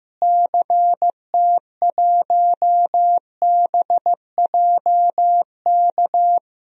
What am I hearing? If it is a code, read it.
Morse code, heard as a keyed tone, about 15 wpm, 700 Hz. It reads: CT1BJK
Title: DXCC-100 Call Signs - 6 Characters - Repeated 2x - 15wpm